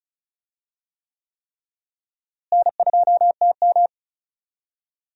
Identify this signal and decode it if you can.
Morse code, heard as a keyed tone, about 36 wpm, 700 Hz. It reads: N2TM
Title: USA Call Signs - 4 Characters - Repeated 3x - 35wpm